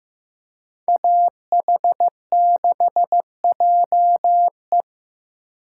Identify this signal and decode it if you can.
Morse code, heard as a keyed tone, about 15 wpm, 700 Hz. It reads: AH6JE
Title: USA Call Signs - 5 Characters - 3x Character Spacing - 15wpm